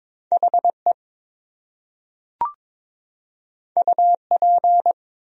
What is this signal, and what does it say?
Morse code, heard as a keyed tone, about 21 wpm, 700 Hz. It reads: HE UP
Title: Top 100 Words Repeated 22wpm